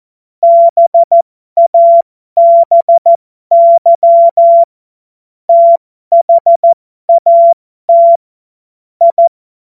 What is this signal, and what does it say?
Morse code, heard as a keyed tone, about 14 wpm, spaced as for 12 wpm, 680 Hz. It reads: BABY THAT I
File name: Morsekurs / Witze